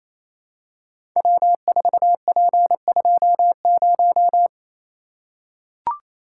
Morse code, heard as a keyed tone, about 29 wpm, 700 Hz. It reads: W4P20
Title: Code Groups - 5 Letter:Numbers - 28wpm